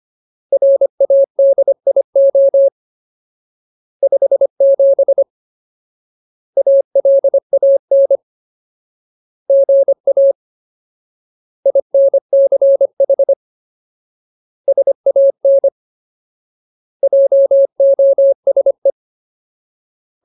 Morse code, heard as a keyed tone, about 25 wpm, 550 Hz. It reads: RADIO 57 ALAN GA INCH SAN JOSE
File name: Ultimate Word List / 25WPM Ultimate